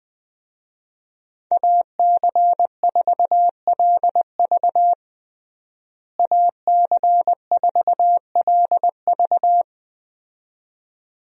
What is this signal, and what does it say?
Morse code, heard as a keyed tone, about 21 wpm, 700 Hz. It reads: AC4LV AC4LV
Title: USA Call Signs - 5 Characters - Repeated 2x - 20wpm